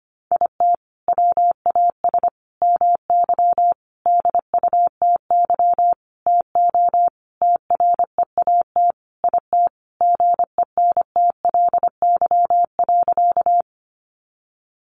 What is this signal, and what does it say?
Morse code, heard as a keyed tone, about 25 wpm, 700 Hz. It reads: IT WAS MY DUTY TO TREAT IT GENTLY.
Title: Sentences from Top 1500 Words 25wpm